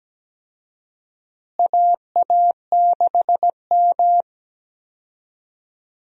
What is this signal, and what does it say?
Morse code, heard as a keyed tone, about 17 wpm, 700 Hz. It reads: AA6M
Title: USA Call Signs - 4 Characters - Repeated 3x - 17wpm